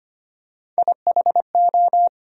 Morse code, heard as a keyed tone, about 25 wpm, 700 Hz. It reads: IHO
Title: USA Call Sign Postfixes - 3 Letters 25wpm